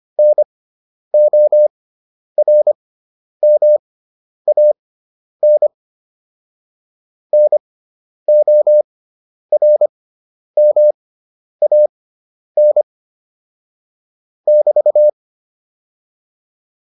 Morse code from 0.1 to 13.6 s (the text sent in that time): NORMAN NORMAN